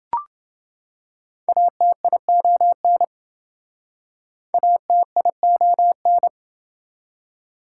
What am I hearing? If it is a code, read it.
Morse code, heard as a keyed tone, about 27 wpm, 700 Hz. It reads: ATION ATION